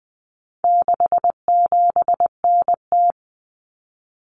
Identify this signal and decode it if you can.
Morse code, heard as a keyed tone, about 20 wpm, 700 Hz. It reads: 67NT
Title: Common Cut Numbers - 4-Digit 20wpm